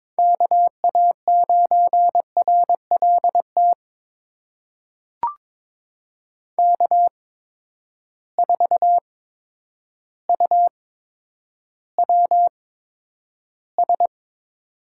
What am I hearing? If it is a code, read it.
Morse code, heard as a keyed tone, about 22 wpm, spaced as for 10 wpm, 700 Hz. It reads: KA9RLT K4UWS
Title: USA Call Signs - 8x Character Spacing - 22wpm